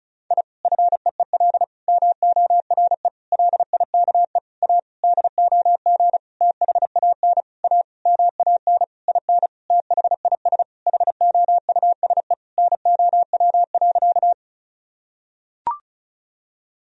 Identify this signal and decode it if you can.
Morse code, heard as a keyed tone, about 34 wpm, 700 Hz. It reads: I FEEL MORE LIKE A DOG THAN A MAN IN THIS HOUSE NOW.
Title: Sentences from Top 1000 Words - Review Part 2 - 35wpm